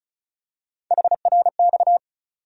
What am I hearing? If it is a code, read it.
Morse code, heard as a keyed tone, about 34 wpm, 700 Hz. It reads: HRX
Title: USA Call Sign Postfixes - 3 Letters 35wpm